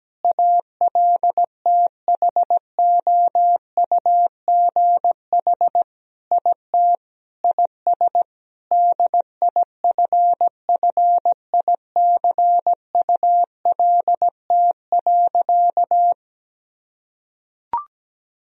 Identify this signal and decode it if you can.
Morse code, heard as a keyed tone, about 17 wpm, 700 Hz. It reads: ALTHOUGH IT IS DIFFICULT.
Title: Sentences from Top 1000 Words - Review Part 3 - 17wpm